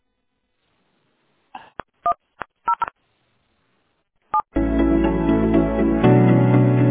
DTMF, heard as telephone touch tones, tones of about 80 ms. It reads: 100 7
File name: 69dffcd9-b7f7-4347-91e7